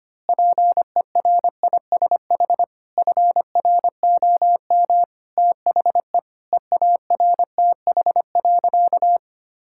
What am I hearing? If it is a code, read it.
Morse code, heard as a keyed tone, about 25 wpm, 700 Hz. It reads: PERISH FROM THE EARTH.
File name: practice / 25wpm / Sentences with 9-Word Phrases - 25wpm